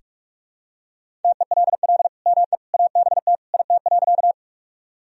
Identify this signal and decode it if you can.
Morse code, heard as a keyed tone, about 43 wpm, 700 Hz. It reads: TELL ME ABT IT.
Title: Sentences from Top 200 Words and Abbreviations 45wpm